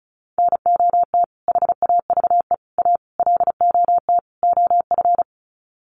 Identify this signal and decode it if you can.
Morse code, heard as a keyed tone, about 34 wpm, 700 Hz. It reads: NOT HAVE A LOT OF
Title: Sets of 5 Words 35wpm